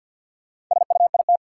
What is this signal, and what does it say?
Morse code, heard as a keyed tone, about 52 wpm, 700 Hz. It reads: SUIT
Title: Sentences from Top 1600 Words - New Word Drill - 50wpm